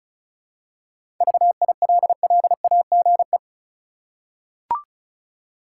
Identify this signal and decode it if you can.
Morse code, heard as a keyed tone, about 36 wpm, 700 Hz. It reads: VILLAGE
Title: Seven Letter Words - 35wpm